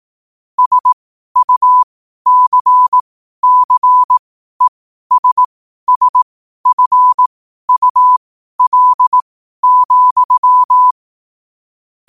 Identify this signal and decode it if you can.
Morse code, heard as a keyed tone, about 18 wpm, spaced as for 12 wpm, 1000 Hz. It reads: SUCCESSFUL,
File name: qotdfiles / 12wpm